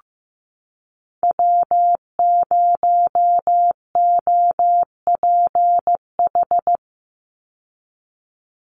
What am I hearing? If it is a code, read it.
Morse code, heard as a keyed tone, about 15 wpm, 700 Hz. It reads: W0OPH
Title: USA Call Signs 15wpm